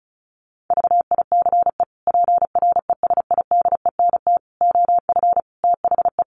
Morse code, heard as a keyed tone, about 34 wpm, 700 Hz. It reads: VICE PRESIDENT OF THE
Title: Sets of 5 Words Encore2 35wpm